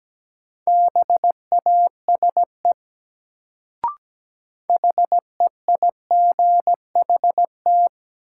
Morse code, heard as a keyed tone, about 17 wpm, 700 Hz. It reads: BASE HEIGHT
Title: Sets of 5 Words - New Word Drill - 17wpm